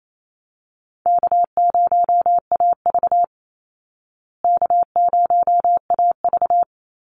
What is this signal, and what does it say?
Morse code, heard as a keyed tone, about 28 wpm, 700 Hz. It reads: K0AV K0AV
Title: Easy Call Signs Repeated 3x - 28wpm